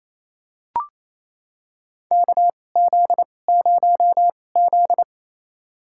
Morse code, heard as a keyed tone, about 28 wpm, spaced as for 20 wpm, 700 Hz. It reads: KZ0Z
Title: USA Call Signs - 4 Characters - 2x Character Spacing - 28wpm